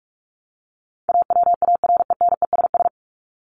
Morse code, heard as a keyed tone, about 44 wpm, 700 Hz. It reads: AWARENESS
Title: Top 1501-1750 Words 45wpm